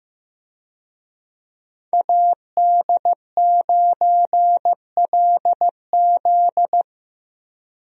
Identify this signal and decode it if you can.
Morse code, heard as a keyed tone, about 16 wpm, 700 Hz. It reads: AD9LZ
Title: USA Call Signs - 5 Characters - Repeated 3x - 15wpm